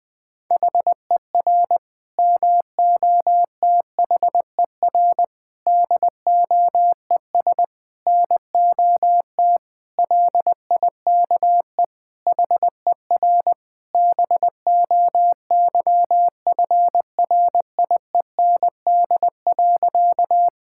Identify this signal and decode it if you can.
Morse code, heard as a keyed tone, about 20 wpm, 700 Hz. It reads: HER MOTHER DOES NOT LIKE HER BOYFRIEND.
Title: Sentences from Top 1000 Words - Review Part 5 - 20wpm